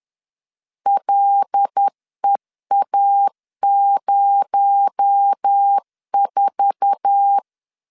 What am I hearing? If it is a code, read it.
Morse code, heard as a keyed tone, about 10 wpm, 800 Hz. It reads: LEA04
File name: CW / DET / 1 9 1500C S B 10wpm 4439